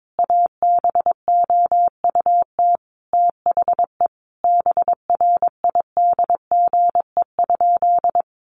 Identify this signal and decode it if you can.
Morse code, heard as a keyed tone, about 22 wpm, 700 Hz. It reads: ABOUT THE BRIDGE?
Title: Sentences from Top 1200 Words 22wpm